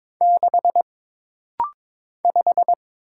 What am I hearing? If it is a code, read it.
Morse code, heard as a keyed tone, about 22 wpm, 700 Hz. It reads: 6 5
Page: This rapid-fire timing is geared to push your ICR (Instant Character Recognition) skills.